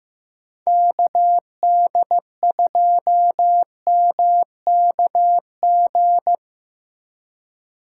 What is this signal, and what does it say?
Morse code, heard as a keyed tone, about 17 wpm, 700 Hz. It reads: KD2MKG
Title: Speed Racing - USA 2x3 Call Signs 15wpm